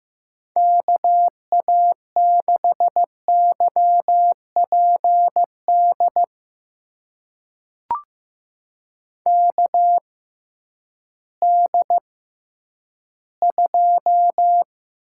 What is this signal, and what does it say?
Morse code, heard as a keyed tone, about 15 wpm, spaced as for 10 wpm, 700 Hz. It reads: KA6YPD KD2
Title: USA Call Signs - 6 Characters - 6x Character Spacing - 15wpm